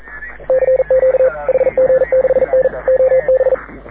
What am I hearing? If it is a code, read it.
Morse code, heard as a keyed tone, about 42 wpm, 530 Hz. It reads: CQ HG6N OB
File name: AUDIO2019 / HG6N CW SSB MIX